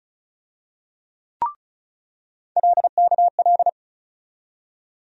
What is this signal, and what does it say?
Morse code, heard as a keyed tone, about 34 wpm, 700 Hz. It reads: LKL
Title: USA Call Sign Postfixes - 3 Letters 35wpm